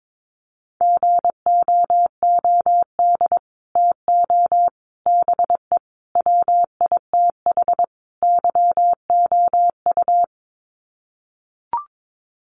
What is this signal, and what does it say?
Morse code, heard as a keyed tone, about 22 wpm, 700 Hz. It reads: GOOD TO BE WITH YOU
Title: Sets of 5 Words Encore2 22wpm